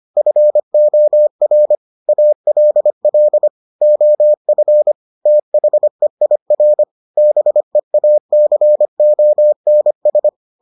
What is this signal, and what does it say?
Morse code, heard as a keyed tone, about 25 wpm, 600 Hz. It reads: FOR ALL OF THEIR BEACONS